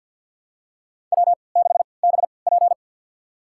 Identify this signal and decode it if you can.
Morse code, heard as a keyed tone, about 52 wpm, spaced as for 29 wpm, 700 Hz. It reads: W6BP
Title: USA Call Signs - 3x Character Spacing - 50wpm